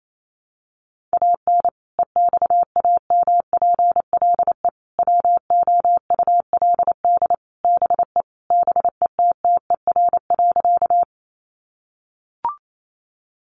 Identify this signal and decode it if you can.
Morse code, heard as a keyed tone, about 28 wpm, 700 Hz. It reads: AN EXAMPLE WOULD BE BETTER.
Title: Sentences from Top 400 Words 28wpm